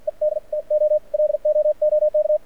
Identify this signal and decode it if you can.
Morse code, heard as a keyed tone, about 52 wpm, spaced as for 45 wpm, 590 Hz. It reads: ED TO LOOK
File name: archived / CRSnet